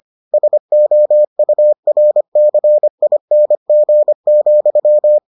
Morse code, heard as a keyed tone, about 25 wpm, 600 Hz. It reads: SOURCING,